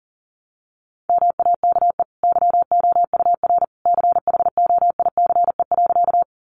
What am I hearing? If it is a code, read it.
Morse code, heard as a keyed tone, about 37 wpm, 700 Hz. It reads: MAKE YOUR CHOICE.
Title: Sentences from Top 900 Words 40wpm